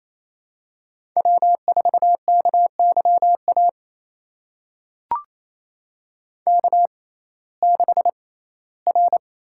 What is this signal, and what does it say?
Morse code, heard as a keyed tone, about 28 wpm, spaced as for 17 wpm, 700 Hz. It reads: W4KYA K6R
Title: USA Call Signs - 5 Characters - 6x Character Spacing - 28wpm